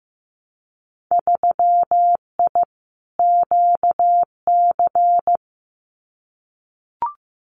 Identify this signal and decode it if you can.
Morse code, heard as a keyed tone, about 15 wpm, 700 Hz. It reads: 3I QC
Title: Winter Field Day Exchanges Encore 15wpm